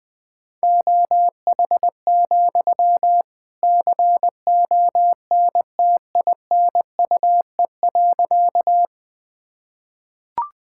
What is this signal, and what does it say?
Morse code, heard as a keyed tone, about 20 wpm, 700 Hz. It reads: OH, CONTINUE.
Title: Sentences from Top 1000 Words - Review Part 6 - 20wpm